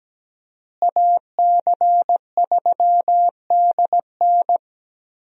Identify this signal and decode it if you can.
Morse code, heard as a keyed tone, about 17 wpm, 700 Hz. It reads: AC3DN
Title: USA Call Signs - 5 Characters - Repeated 3x - 17wpm